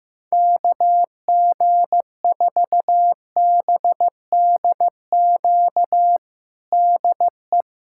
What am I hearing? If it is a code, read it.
Morse code, heard as a keyed tone, about 15 wpm, 700 Hz. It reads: KG4BDQ DE
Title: USA Callsign DE Callsign 15wpm